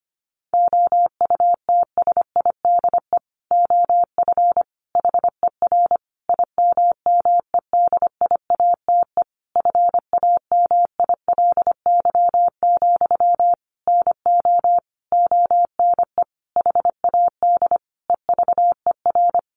Morse code, heard as a keyed tone, about 25 wpm, 700 Hz. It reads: OUTSIDE OF HER IMMEDIATE FAMILY, NO ONE HAD EVER
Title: Sentences from Top 1200 Words 25wpm